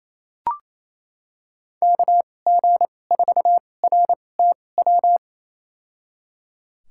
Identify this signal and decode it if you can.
Morse code, heard as a keyed tone, about 28 wpm, spaced as for 20 wpm, 700 Hz. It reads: KG4RTW
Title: USA Call Signs - 6 Characters - 2x Character Spacing - 28wpm